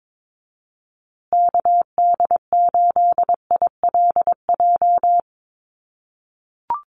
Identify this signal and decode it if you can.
Morse code, heard as a keyed tone, about 22 wpm, 700 Hz. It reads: KD8ILJ
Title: USA Call Signs Repeated 3x 22wpm